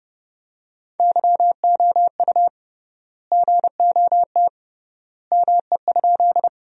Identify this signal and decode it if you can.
Morse code, heard as a keyed tone, about 30 wpm, 700 Hz. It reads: YOU GOT ME?
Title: 3x Word-Spacing - Sentences from Top 300 Words - 30wpm